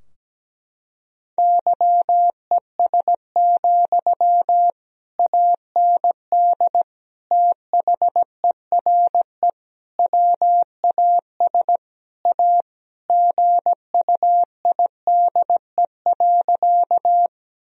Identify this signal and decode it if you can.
Morse code, heard as a keyed tone, about 17 wpm, 700 Hz. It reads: YES, AND THERE WAS A GUIDE.
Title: Sentences from Top 1000 Words - Review Part 3 - 17wpm